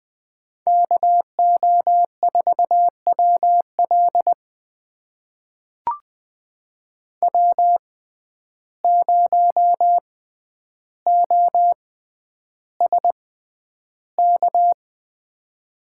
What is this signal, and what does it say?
Morse code, heard as a keyed tone, about 20 wpm, spaced as for 10 wpm, 700 Hz. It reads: KO4WL W0OSK
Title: USA Call Signs - 5 Characters - 6x Character Spacing - 20wpm